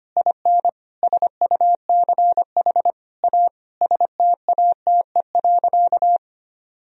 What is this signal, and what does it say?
Morse code, heard as a keyed tone, about 25 wpm, 700 Hz. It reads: IN SUCH A STATE.
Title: Sentences from Top 1000 Words - Review Part 2 - 25wpm